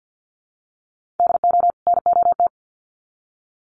Morse code, heard as a keyed tone, about 50 wpm, 700 Hz. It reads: DO NOT
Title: Sets of 2 Words 50wpm